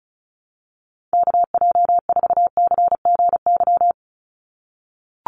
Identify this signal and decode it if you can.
Morse code, heard as a keyed tone, about 34 wpm, 700 Hz. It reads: KJ4CGY
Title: USA Call Signs 35wpm